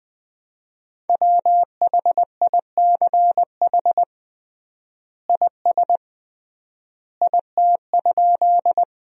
Morse code, heard as a keyed tone, about 21 wpm, 700 Hz. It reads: WHICH IS IT?
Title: 3x Word-Spacing - Sentences from Top 100 Words - 20wpm